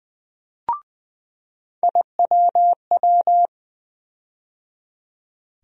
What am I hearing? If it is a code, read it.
Morse code, heard as a keyed tone, about 20 wpm, 700 Hz. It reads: IWW
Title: USA Call Sign Postfixes - 3 Letters 20wpm